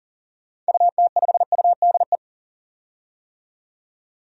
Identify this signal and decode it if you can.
Morse code, heard as a keyed tone, about 37 wpm, 700 Hz. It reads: UT5UDE
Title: DXCC-100 Call Signs - 6 Characters - 40wpm